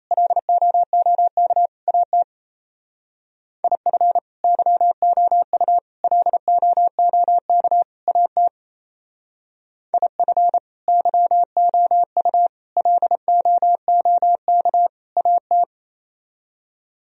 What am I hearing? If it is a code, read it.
Morse code, heard as a keyed tone, about 32 wpm, 700 Hz. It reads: LOOK AT IF YOU LOOK AT IF YOU LOOK AT
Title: Speed-Racing - Sets of 4 Words 28wpm